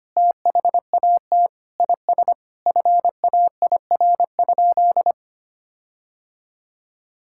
Morse code, heard as a keyed tone, about 25 wpm, 700 Hz. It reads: THAT IS FAIR?
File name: practice / 25wpm / Sentences with 4-Word Phrases - 25wpm